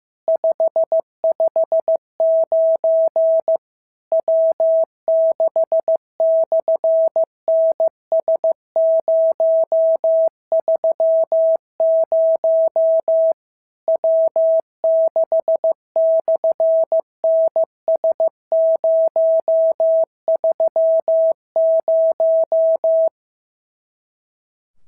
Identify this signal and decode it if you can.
Morse code, heard as a keyed tone, about 15 wpm, 650 Hz. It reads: 559 W6/NS030 W6/NS030
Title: SOTA QSO Practice 15wpm